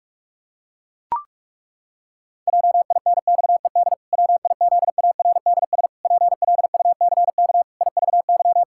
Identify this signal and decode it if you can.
Morse code, heard as a keyed tone, about 41 wpm, 700 Hz. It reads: JINXED WIZARDS PLUCK IVY
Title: Pangram Challenge v2 - 45wpm